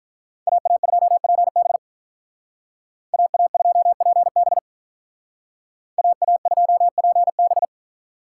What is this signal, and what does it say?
Morse code, heard as a keyed tone, about 45 wpm, 700 Hz. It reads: AA2PB AA2PB AA2PB